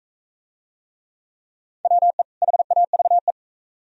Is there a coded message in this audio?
Yes: Morse code from 1.8 to 3.3 s, heard as a keyed tone, about 41 wpm, 700 Hz. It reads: WE HAVE